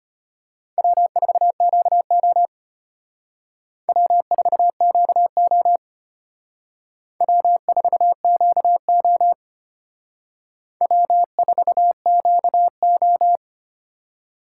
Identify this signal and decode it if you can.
Morse code, heard as a keyed tone, about 31 wpm, 700 Hz. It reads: W4QO W4QO W4QO W4QO
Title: Speed Racing - USA 1x2 Call Signs 25wpm